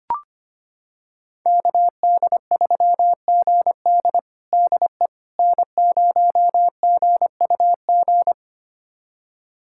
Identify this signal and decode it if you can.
Morse code, heard as a keyed tone, about 25 wpm, 700 Hz. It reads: KD3GD DE N0GUG
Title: USA Callsign DE Callsign - 5 Characters - 25wpm